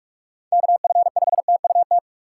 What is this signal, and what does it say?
Morse code, heard as a keyed tone, about 46 wpm, 700 Hz. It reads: KF5TUT
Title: USA Call Signs - 6 Characters - 3x Character Spacing - 45wpm